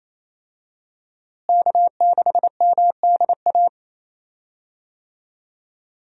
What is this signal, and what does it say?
Morse code, heard as a keyed tone, about 28 wpm, 700 Hz. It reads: K6MDA
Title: USA Call Signs - 5 Characters - Repeated 3x - 28wpm